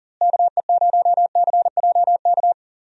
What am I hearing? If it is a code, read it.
Morse code, heard as a keyed tone, about 39 wpm, 700 Hz. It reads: KE0CJK
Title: USA Call Signs - 6 Characters - Repeated 2x - 40wpm